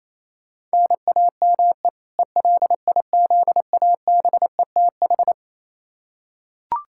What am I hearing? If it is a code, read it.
Morse code, heard as a keyed tone, about 28 wpm, 700 Hz. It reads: NAME ELIZABETH
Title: K1USN SST QSO Practice 28wpm